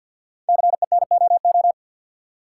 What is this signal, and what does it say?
Morse code, heard as a keyed tone, about 50 wpm, 700 Hz. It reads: CENOO
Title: Code Groups - 5 Letters - 50wpm